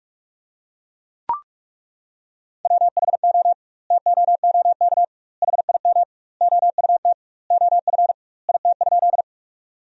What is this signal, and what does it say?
Morse code, heard as a keyed tone, about 41 wpm, 700 Hz. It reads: WHO TOOK HIM OUT OF IT?
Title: 2x Word-Spacing - Sentences from Top 300 Words - 45wpm